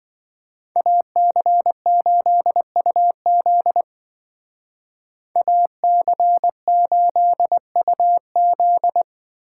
Morse code, heard as a keyed tone, about 25 wpm, 700 Hz. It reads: AC8UZ AC8UZ